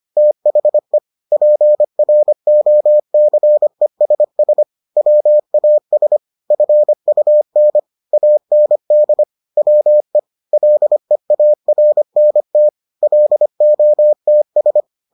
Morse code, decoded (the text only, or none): THE PROCESS WAS FUN AND WE LEARNT LOTS